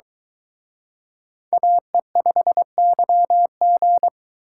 Morse code, heard as a keyed tone, about 22 wpm, 700 Hz. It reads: AE5YG